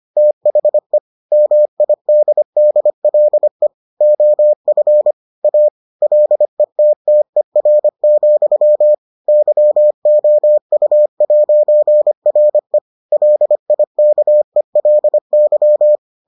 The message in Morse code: THE MIDDLE OF A LETTER, YOU'RE LIKELY

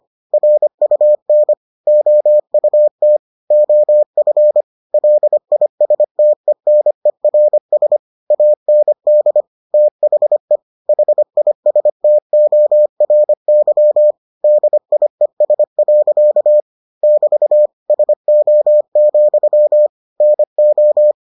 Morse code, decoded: RUN OUT OF LISTENERS AND THE HISTORY DIES. <BT> SO, NO

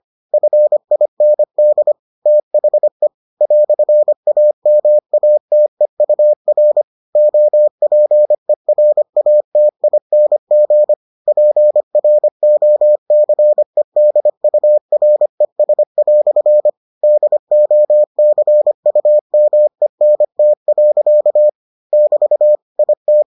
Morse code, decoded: FIND THE "AMATEUR OPERATING PROCEDURES" DOCUMENT. <BT> IT

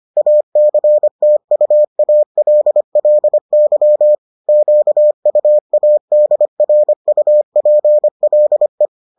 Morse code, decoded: ACTUALLY QUADRUPLE

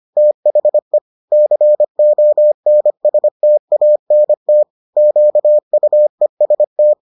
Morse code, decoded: THE CONSTANT QUEST

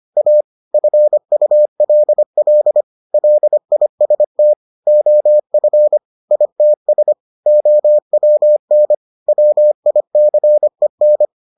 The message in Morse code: A FULL LIST OF ITS OWN WICEN

